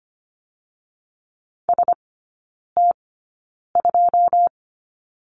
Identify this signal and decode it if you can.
Morse code, heard as a keyed tone, about 25 wpm, spaced as for 9 wpm, 690 Hz. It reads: ST2
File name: Lesson 14e - Number 2 - Three Characters 25wpm